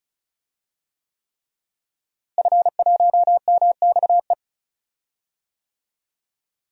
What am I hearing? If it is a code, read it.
Morse code, heard as a keyed tone, about 34 wpm, 700 Hz. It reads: F1MXE
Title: DXCC-100 Call Signs - 5 Characters - Repeated 3x - 35wpm